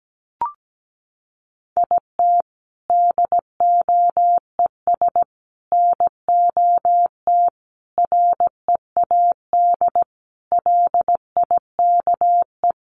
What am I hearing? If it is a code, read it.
Morse code, heard as a keyed tone, about 17 wpm, 700 Hz. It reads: IT DOES NOT READ LIKE
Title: Sentences from Top 800 Words 17wpm